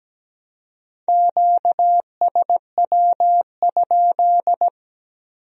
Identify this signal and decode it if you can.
Morse code, heard as a keyed tone, about 20 wpm, 700 Hz. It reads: QSW?
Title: Speed-Racing Q-Codes - 17wpm